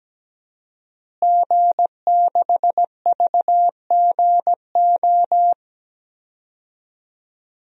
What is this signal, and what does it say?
Morse code, heard as a keyed tone, about 17 wpm, 700 Hz. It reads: G6VGO
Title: DXCC-100 Call Signs - 5 Characters - Repeated 2x - 17wpm